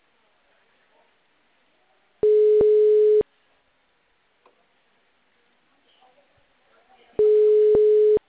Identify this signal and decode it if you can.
Ringback tone (Europe, 425 Hz), with speech.
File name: out- -2015- - - .18015